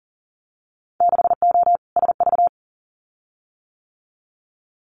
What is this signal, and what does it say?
Morse code, heard as a keyed tone, about 39 wpm, 700 Hz. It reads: 6O SV
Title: Winter Field Day Exchanges Encore 40wpm